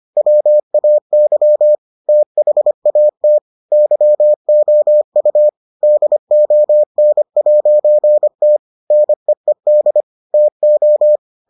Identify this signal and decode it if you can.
Morse code, heard as a keyed tone, about 25 wpm, 600 Hz. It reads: WAY THAT YOU DON'T NEED TO